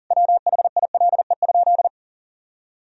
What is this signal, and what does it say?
Morse code, heard as a keyed tone, about 40 wpm, 700 Hz. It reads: WHILE?
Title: Sentences from Top 1000 Words - Review Part 1 - 40wpm